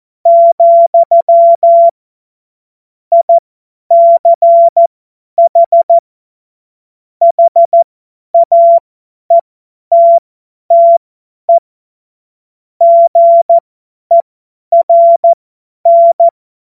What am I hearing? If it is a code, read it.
Morse code, heard as a keyed tone, about 14 wpm, spaced as for 10 wpm, 680 Hz. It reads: , ICH HAETTE GERN
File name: Morsekurs / Witze / Witz 16 Januar 2024 50 F70